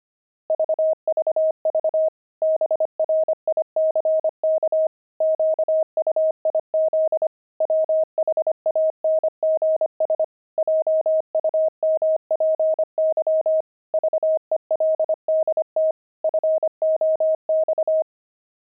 Morse code, heard as a keyed tone, about 25 wpm, 630 Hz. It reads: VVV BRICK QUIZ WHANGS JUMPY VELDT FOX
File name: pangram-fs-25-cs-25-veldt